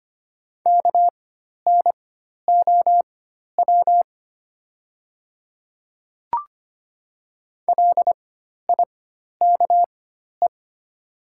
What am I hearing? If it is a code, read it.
Morse code, heard as a keyed tone, about 25 wpm, spaced as for 12 wpm, 700 Hz. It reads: KNOW LIKE
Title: 4x Character Spacing - Top 200 Words - 25wpm